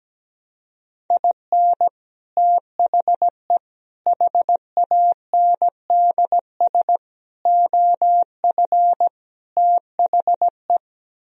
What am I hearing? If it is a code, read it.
Morse code, heard as a keyed tone, about 17 wpm, 700 Hz. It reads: IN THE HANDS OF THE
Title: Speed-Racing - Sets of 5 Words 17wpm